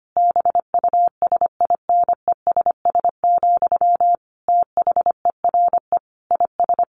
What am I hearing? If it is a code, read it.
Morse code, heard as a keyed tone, about 25 wpm, 700 Hz. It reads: BUSINESS, THERE IS
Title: Sentences from Top 1400 Words 25wpm